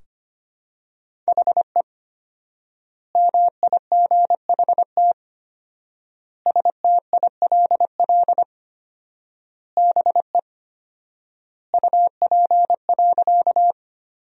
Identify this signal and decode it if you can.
Morse code, heard as a keyed tone, about 25 wpm, 700 Hz. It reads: HE MIGHT STILL BE UP.
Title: 4x Word-Spacing - Sentences from Top 300 Words - 25wpm